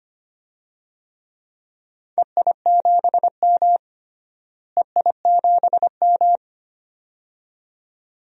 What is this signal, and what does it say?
Morse code, heard as a keyed tone, about 25 wpm, 700 Hz. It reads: EI7M EI7M
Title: DXCC-100 Call Signs - 4 Characters - Repeated 2x - 25wpm